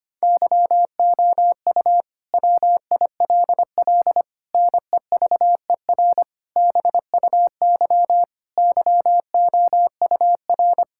Morse code, heard as a keyed tone, about 25 wpm, 700 Hz. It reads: YOU WILL NEVER BUY YOUR
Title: Sentences from Top 1000 Words - Review Part 5 - 25wpm